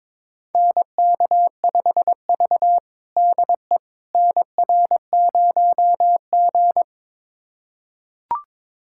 Morse code, heard as a keyed tone, about 22 wpm, 700 Hz. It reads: NK5V DE NR0G
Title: USA Callsign DE Callsign - 4 Characters - 22wpm